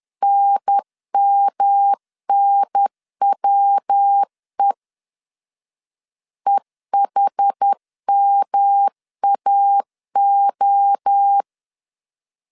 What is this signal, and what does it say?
Morse code, heard as a keyed tone, about 10 wpm, 800 Hz. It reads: NMNWE EHMAO
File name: CW / DET / 1 3 1500C S B 10wpm 3335